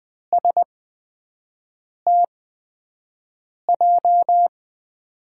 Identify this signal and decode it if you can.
Morse code, heard as a keyed tone, about 20 wpm, spaced as for 5 wpm, 700 Hz. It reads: STJ
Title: USA Call Signs - 6 Characters - 8x Character Spacing - 20wpm